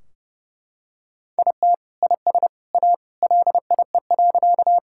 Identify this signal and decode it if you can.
Morse code, heard as a keyed tone, about 30 wpm, 700 Hz. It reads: IT IS A LIE.
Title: Sentences from Top 1600 Words 30wpm